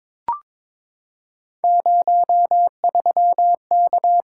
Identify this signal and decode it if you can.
Morse code, heard as a keyed tone, about 22 wpm, 700 Hz. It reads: 03K
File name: practice / 22wpm / Three Letter-Number Combinations - 22wpm